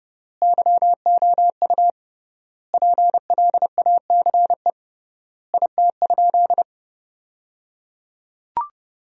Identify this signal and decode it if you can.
Morse code, heard as a keyed tone, about 30 wpm, 700 Hz. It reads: YOU PLACE IT?
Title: 3x Word-Spacing - Sentences from Top 200 Words - 30wpm